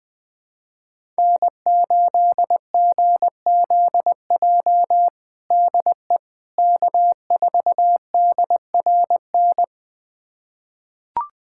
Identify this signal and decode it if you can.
Morse code, heard as a keyed tone, about 20 wpm, 700 Hz. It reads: N8GZJ DE K4DRN
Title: USA Callsign DE Callsign 20wpm